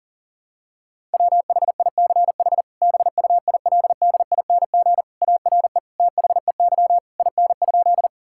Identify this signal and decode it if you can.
Morse code, heard as a keyed tone, about 37 wpm, 700 Hz. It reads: WHICH BUILDING ARE THEY IN?
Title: Sentences from Top 1000 Words - Review Part 5 - 40wpm